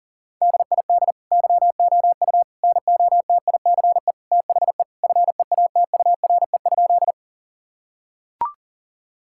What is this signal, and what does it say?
Morse code, heard as a keyed tone, about 40 wpm, 700 Hz. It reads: DID YOU NOTICE THE FEATURE?
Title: Sentences from Top 1000 Words - Review Part 5 - 40wpm